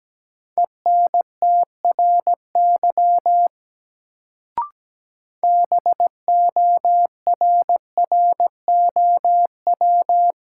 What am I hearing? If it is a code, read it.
Morse code, heard as a keyed tone, about 17 wpm, 700 Hz. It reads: ENTRY BORROW
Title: Sentences from Top 1500 Words - New Word Drill Alt - 17wpm